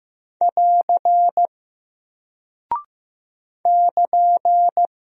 Morse code, heard as a keyed tone, about 15 wpm, 700 Hz. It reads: <AR> <KN>
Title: Single Character (Letter:Number:Punctuation:Prosign) Alt 15wpm